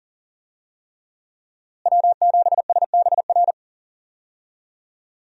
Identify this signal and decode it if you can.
Morse code, heard as a keyed tone, about 39 wpm, 700 Hz. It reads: W7SBR
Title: USA Call Signs - 5 Characters - Repeated 3x - 40wpm